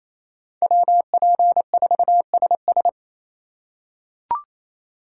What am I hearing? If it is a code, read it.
Morse code, heard as a keyed tone, about 28 wpm, 700 Hz. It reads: WP4SS
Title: USA Call Signs - 3x Character Spacing - 28wpm